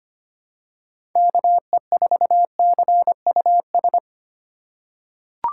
Morse code, heard as a keyed tone, about 25 wpm, 700 Hz. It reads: KE4CUS
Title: USA Call Signs - 4x Character Spacing - 25wpm